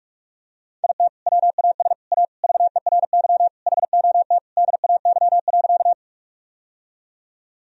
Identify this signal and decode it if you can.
Morse code, heard as a keyed tone, about 42 wpm, 700 Hz. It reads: IT WAS A VERY HOT DAY.
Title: Sentences from Top 1000 Words - Review Part 3 - 45wpm